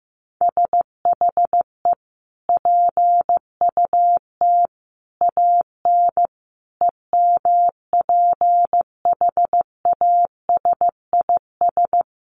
Morse code, heard as a keyed tone, about 15 wpm, 700 Hz. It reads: SHE PUT AN EMPHASIS
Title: Sentences from Top 1400 Words 15wpm